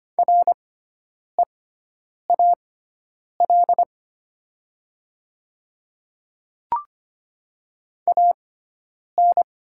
Morse code, heard as a keyed tone, about 25 wpm, spaced as for 9 wpm, 700 Hz. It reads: REAL AN
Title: 6x Character Spacing - Top 500 Words - 25wpm